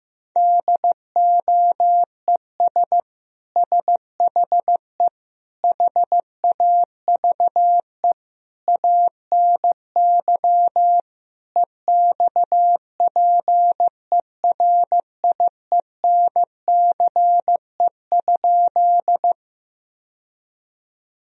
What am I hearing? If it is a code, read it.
Morse code, heard as a keyed tone, about 15 wpm, 700 Hz. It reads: DOES SHE HAVE ANY EXPERIENCE?
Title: Sentences from Top 1000 Words - Review Part 3 - 15wpm